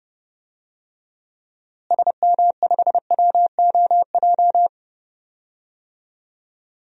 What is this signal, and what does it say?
Morse code, heard as a keyed tone, about 30 wpm, 700 Hz. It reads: SM5WOJ
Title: DXCC-100 Call Signs - 6 Characters - Repeated 3x - 30wpm